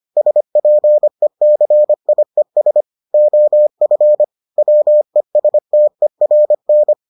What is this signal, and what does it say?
Morse code, heard as a keyed tone, about 25 wpm, 600 Hz. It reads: SPECIES OF WESTERN